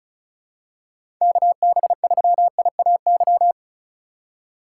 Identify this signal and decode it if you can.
Morse code, heard as a keyed tone, about 36 wpm, 700 Hz. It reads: KB3IAY
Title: USA Call Signs - 6 Characters - 3x Character Spacing - 35wpm